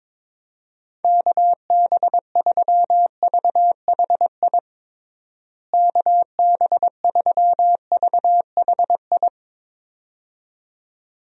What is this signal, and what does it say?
Morse code, heard as a keyed tone, about 22 wpm, 700 Hz. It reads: KB3VHI KB3VHI
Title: USA Call Signs - 6 Characters - Repeated 2x - 22wpm